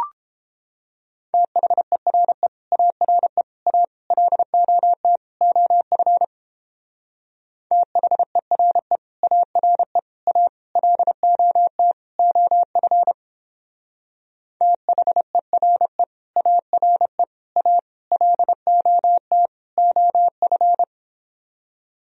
Morse code, heard as a keyed tone, about 29 wpm, 700 Hz. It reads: THERE ARE A LOT OF THERE ARE A LOT OF THERE ARE A LOT OF